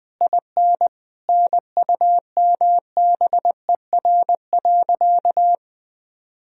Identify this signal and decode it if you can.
Morse code, heard as a keyed tone, about 20 wpm, 700 Hz. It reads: IN NUMBER.
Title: Sentences from Top 1000 Words - Review Part 4 - 20wpm